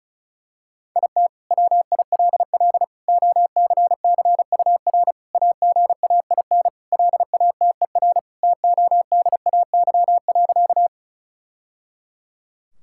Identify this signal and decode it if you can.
Morse code, heard as a keyed tone, about 34 wpm, 700 Hz. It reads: IT WILL OCCUR AGAIN LATER TODAY.
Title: Sentences from Top 1000 Words - Review Part 6 - 35wpm